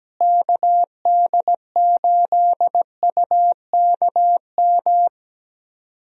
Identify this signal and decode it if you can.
Morse code, heard as a keyed tone, about 17 wpm, 700 Hz. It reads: KD8UKM
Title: USA Call Signs - 6 Characters - Repeated 2x - 17wpm